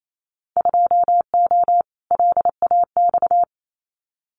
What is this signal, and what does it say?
Morse code, heard as a keyed tone, about 28 wpm, 700 Hz. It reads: 2O LAX
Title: Winter Field Day Exchanges 28wpm